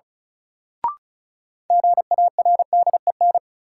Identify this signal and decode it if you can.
Morse code, heard as a keyed tone, about 36 wpm, 700 Hz. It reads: GARDEN
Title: Sentences from Top 700 Words - New Word Drill - 35wpm